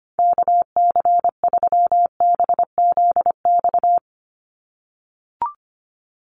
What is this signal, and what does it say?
Morse code, heard as a keyed tone, about 25 wpm, 700 Hz. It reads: KC3BZX
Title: USA Call Signs 25wpm